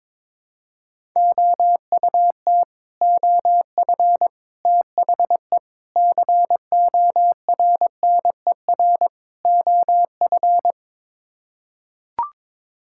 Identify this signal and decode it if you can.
Morse code, heard as a keyed tone, about 21 wpm, 700 Hz. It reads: OUT OF THE CORNER OF
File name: practice / 22wpm / Build A Phrase - 5-Word Phrases - 22wpm